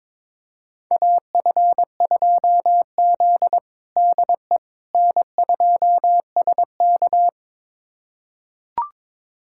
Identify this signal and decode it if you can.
Morse code, heard as a keyed tone, about 22 wpm, 700 Hz. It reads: AF2Z DE N2SK
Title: USA Callsign DE Callsign - 4 Characters - 22wpm